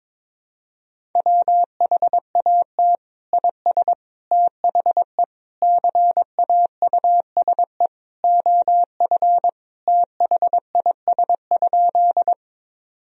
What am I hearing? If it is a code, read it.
Morse code, heard as a keyed tone, about 22 wpm, 700 Hz. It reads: WHAT IS THE CAUSE OF THIS?
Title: Sentences from Top 1000 Words - Review Part 1 - 22wpm